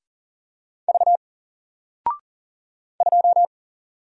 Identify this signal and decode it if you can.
Morse code, heard as a keyed tone, about 40 wpm, 700 Hz. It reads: V 2
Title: Single Character (Letter:Number:Punctuation:Prosign) Alt 40wpm